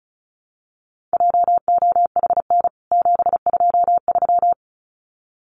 Morse code, heard as a keyed tone, about 34 wpm, 700 Hz. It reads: JOHN 723
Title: CWT Exchange 35wpm